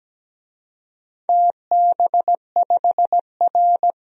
Morse code, heard as a keyed tone, about 17 wpm, 700 Hz. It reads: TB5R
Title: DXCC-100 Call Signs - 4 Characters - 17wpm